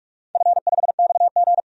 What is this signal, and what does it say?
Morse code, heard as a keyed tone, about 44 wpm, 700 Hz. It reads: F5XG
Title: DXCC-100 Call Signs - 4 Characters - 45wpm